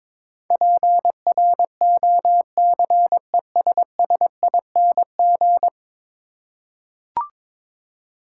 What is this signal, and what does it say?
Morse code, heard as a keyed tone, about 22 wpm, 700 Hz. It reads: PROCESSING
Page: morse-code-ninja-practice-22wpm